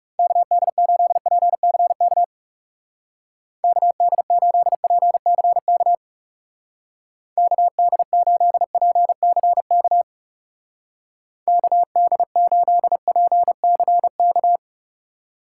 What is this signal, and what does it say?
Morse code, heard as a keyed tone, about 39 wpm, 700 Hz. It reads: KD8PCK KD8PCK KD8PCK KD8PCK